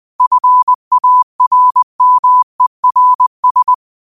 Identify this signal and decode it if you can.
Morse code, heard as a keyed tone, about 20 wpm, 1000 Hz. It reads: FARMERS
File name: 20wpm